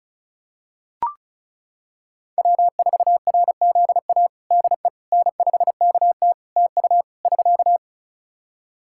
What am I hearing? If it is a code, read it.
Morse code, heard as a keyed tone, about 36 wpm, 700 Hz. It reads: W4RZA DE N5KT TU <SK>
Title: Practice QSO - Exchange 4 - Ending - 35wpm